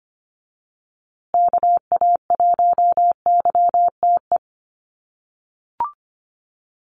Morse code, heard as a keyed tone, about 25 wpm, 700 Hz. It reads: KA1YTE
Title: USA Call Signs Repeated 3x 25wpm